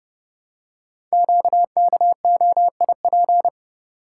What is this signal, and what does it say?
Morse code, heard as a keyed tone, about 30 wpm, 690 Hz. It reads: QKOIP
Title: Code Groups - 5 Letters - 30wpm